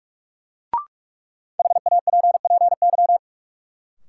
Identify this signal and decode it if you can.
Morse code, heard as a keyed tone, about 46 wpm, 700 Hz. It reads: HAPPY
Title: Sentences from Top 1000 Words - New Word Drill - 45wpm